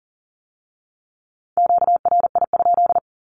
Morse code, heard as a keyed tone, about 38 wpm, 700 Hz. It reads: QRI?
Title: Q-Codes with Spoken Meaning - 40wpm